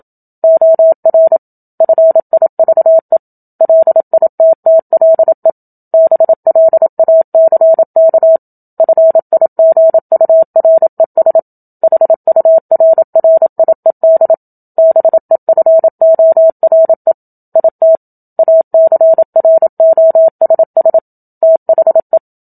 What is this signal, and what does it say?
Morse code, heard as a keyed tone, about 27 wpm, 650 Hz. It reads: OR FIVE LITTLE BLACK FIGURES HURRIED BEFORE IT ACROSS THE
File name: Book Two / 01-Chapter 1 - 27 WPM